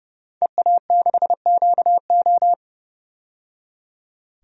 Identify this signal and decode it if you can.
Morse code, heard as a keyed tone, about 30 wpm, 700 Hz. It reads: EA6QO
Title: DXCC-100 Call Signs - 5 Characters - 30wpm